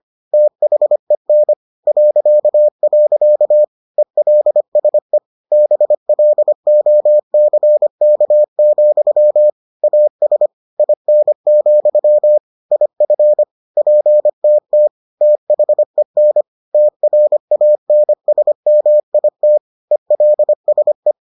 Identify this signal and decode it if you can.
Morse code, heard as a keyed tone, about 25 wpm, 600 Hz. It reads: THEN .. ELSE BLOCK, AS IN, IF PTT THEN TRANSMIT ELSE